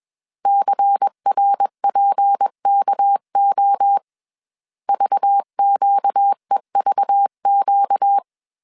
Morse code, heard as a keyed tone, about 21 wpm, 800 Hz. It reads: CRPKO VQEVQ
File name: CW / DET / 1 6 1500C S C 20wpm 2116